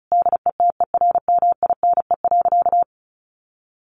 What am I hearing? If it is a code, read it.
Morse code, heard as a keyed tone, about 34 wpm, 700 Hz. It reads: DETERMINE.
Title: Sentences from Top 1500 Words 35wpm